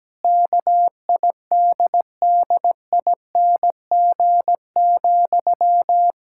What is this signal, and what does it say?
Morse code, heard as a keyed tone, about 17 wpm, 700 Hz. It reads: KIDDING,
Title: Pangram Challenge v2 - 17wpm